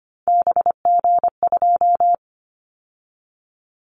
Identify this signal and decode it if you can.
Morse code, heard as a keyed tone, about 25 wpm, 700 Hz. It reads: BG2
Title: Callsign Prefixes for Top 100 DXCC - 25wpm